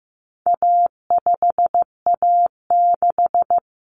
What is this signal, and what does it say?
Morse code, heard as a keyed tone, about 15 wpm, 700 Hz. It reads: A5A6
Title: Common Cut Numbers - 4-Digit 15wpm